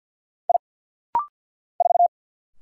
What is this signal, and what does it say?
Morse code, heard as a keyed tone, about 48 wpm, 700 Hz. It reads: I 4
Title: Single Letter-Number - Rapid-Fire 50wpm